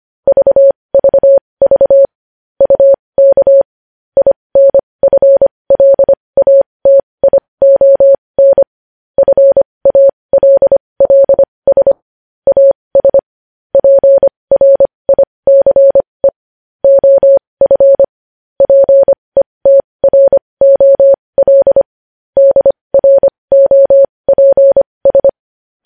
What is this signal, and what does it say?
Morse code, heard as a keyed tone, about 25 wpm, spaced as for 20 wpm, 550 Hz. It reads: VVV UK INFLATION FALLS AS PRICE OF PETROL DROPS
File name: podcast / News headlines-1x-25x20